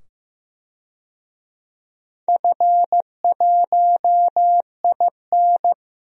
Morse code, heard as a keyed tone, about 15 wpm, 700 Hz. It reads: F1IN
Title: DXCC-100 Call Signs - 4 Characters - 15wpm